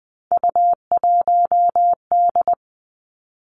Morse code, heard as a keyed tone, about 20 wpm, 700 Hz. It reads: U1D
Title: Three Challenging Letter-Number Combinations 20wpm